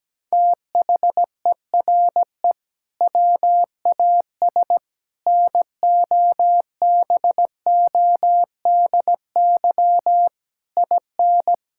Morse code, heard as a keyed tone, about 17 wpm, 700 Hz. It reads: THERE WAS NOBODY IN
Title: Sentences from Top 1600 Words 17wpm